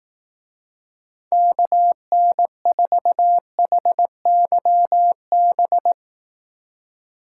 Morse code, heard as a keyed tone, about 18 wpm, 700 Hz. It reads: KN4HYB
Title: Speed Racing - USA 2x3 Call Signs 15wpm